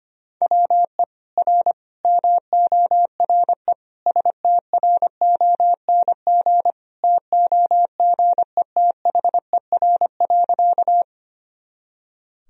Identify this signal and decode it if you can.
Morse code, heard as a keyed tone, about 25 wpm, 700 Hz. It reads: WE R MORE STRONG TOGETHER.
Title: Sentences from Top 500 Words and Abbreviations 25wpm